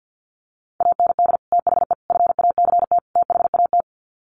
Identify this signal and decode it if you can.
Morse code, heard as a keyed tone, about 50 wpm, 700 Hz. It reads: AND THE FACT THAT
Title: Sets of 4 Words 50wpm